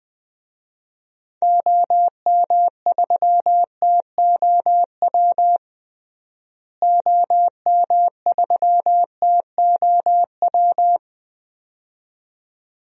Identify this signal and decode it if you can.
Morse code, heard as a keyed tone, about 20 wpm, 700 Hz. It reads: OM3TOW OM3TOW
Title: DXCC-100 Call Signs - 6 Characters - Repeated 2x - 20wpm